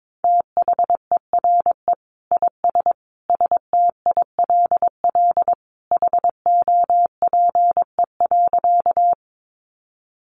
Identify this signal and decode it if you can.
Morse code, heard as a keyed tone, about 22 wpm, 700 Hz. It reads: THERE IS STILL HOPE.
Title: Sentences from Top 1000 Words 22wpm